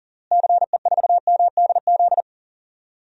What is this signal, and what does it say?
Morse code, heard as a keyed tone, about 40 wpm, 700 Hz. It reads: CE4MDZ
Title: DXCC-100 Call Signs - 6 Characters - Repeated 3x - 40wpm